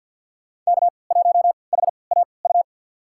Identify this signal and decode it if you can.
Morse code, heard as a keyed tone, about 52 wpm, spaced as for 29 wpm, 700 Hz. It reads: K1HAU
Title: USA Call Signs - 5 Characters - 3x Character Spacing - 50wpm